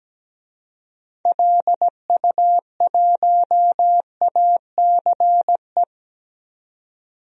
Morse code, heard as a keyed tone, about 17 wpm, 700 Hz. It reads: LU1ACE
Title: DXCC-100 Call Signs - 6 Characters - 17wpm